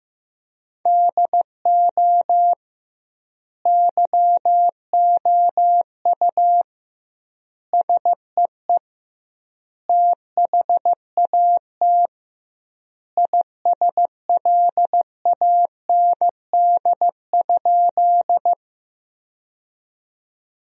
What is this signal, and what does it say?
Morse code, heard as a keyed tone, about 15 wpm, 700 Hz. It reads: DO YOU SEE THAT ISLAND?
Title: 2x Word-Spacing - Sentences from Top 500 Words - 15wpm